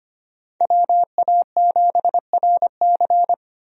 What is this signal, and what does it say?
Morse code, heard as a keyed tone, about 25 wpm, 700 Hz. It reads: WA7RC
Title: USA Call Signs - 5 Characters - 25wpm